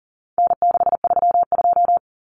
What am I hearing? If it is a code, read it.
Morse code, heard as a keyed tone, about 40 wpm, 700 Hz. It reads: N632
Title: Common Cut Numbers - 4-Digit 40wpm